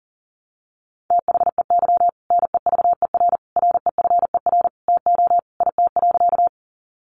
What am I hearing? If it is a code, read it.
Morse code, heard as a keyed tone, about 40 wpm, 700 Hz. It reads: THEY NEVER REFER TO IT.
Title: Sentences from Top 1000 Words 40wpm